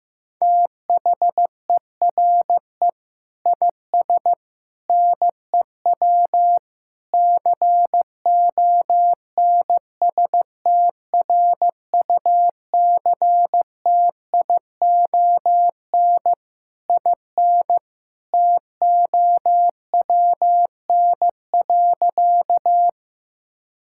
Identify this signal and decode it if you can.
Morse code, heard as a keyed tone, about 15 wpm, 700 Hz. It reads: THERE IS NEW CONSTRUCTION IN TOWN.
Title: Sentences from Top 1600 Words 15wpm